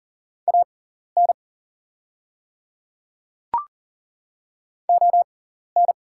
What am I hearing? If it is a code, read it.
Morse code, heard as a keyed tone, about 40 wpm, spaced as for 14 wpm, 700 Hz. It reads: AN ON
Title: 6x Character Spacing - Top 200 Words - 40wpm